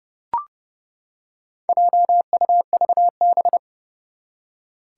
Morse code, heard as a keyed tone, about 30 wpm, 690 Hz. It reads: JUVB
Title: Four Challenging Letter-Number Combinations 30wpm